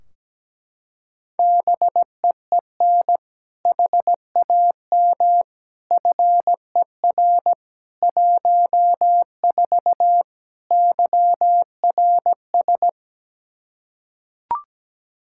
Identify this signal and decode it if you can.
Morse code, heard as a keyed tone, about 17 wpm, 700 Hz. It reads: BEEN HAM FER 14 YRS
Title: Practice QSO - Exchange 3 - More Info - 17wpm